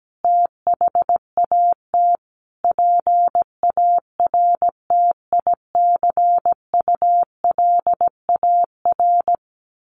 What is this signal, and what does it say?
Morse code, heard as a keyed tone, about 17 wpm, 700 Hz. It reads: THAT PARTICULAR
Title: Sentences from Top 1400 Words 17wpm